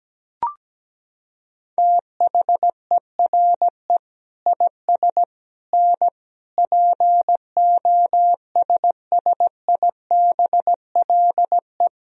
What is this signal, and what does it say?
Morse code, heard as a keyed tone, about 17 wpm, 700 Hz. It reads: THERE IS N POSSIBLE
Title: Sentences from Top 500 Words and Abbreviations 17wpm